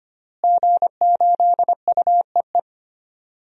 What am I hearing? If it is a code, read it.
Morse code, heard as a keyed tone, about 25 wpm, 700 Hz. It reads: G8UEE
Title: DXCC-100 Call Signs - 5 Characters - Repeated 3x - 25wpm